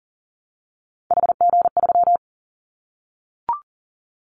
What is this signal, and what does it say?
Morse code, heard as a keyed tone, about 40 wpm, 700 Hz. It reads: HG3
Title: Callsign Prefixes for Top 25 DXCC - Repeated 3x - 40wpm